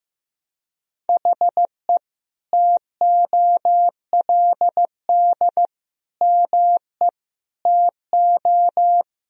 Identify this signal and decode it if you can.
Morse code, heard as a keyed tone, about 15 wpm, 700 Hz. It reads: HE TOLD ME TO
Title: Sentences from Top 1000 Words - Review Part 3 - 15wpm